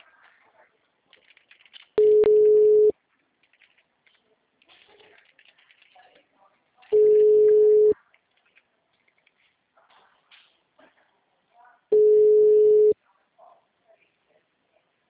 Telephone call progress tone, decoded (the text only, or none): ringback tone